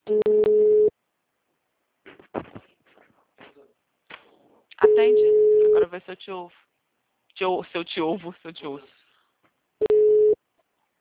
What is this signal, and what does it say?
Ringback tone (Europe, 425 Hz), with speech.
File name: out- -2005- - - .20103